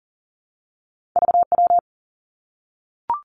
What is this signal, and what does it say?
Morse code, heard as a keyed tone, about 39 wpm, 700 Hz. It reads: VW
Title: Two Challenging Letter-Number Combinations 40wpm